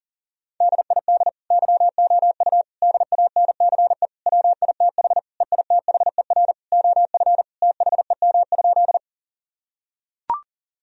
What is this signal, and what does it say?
Morse code, heard as a keyed tone, about 40 wpm, 700 Hz. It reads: DID YOU DANCE WITH EITHER OF THEM?
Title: Sentences from Top 1000 Words - Review Part 4 - 40wpm